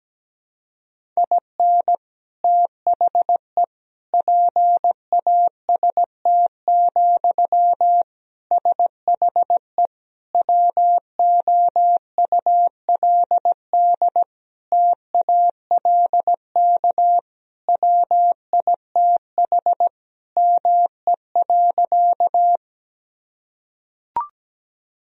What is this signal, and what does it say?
Morse code, heard as a keyed tone, about 17 wpm, 700 Hz. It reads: IN THE PAST, SHE WOULD TALK WITH ME.
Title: Sentences from Top 1000 Words - Review Part 3 - 17wpm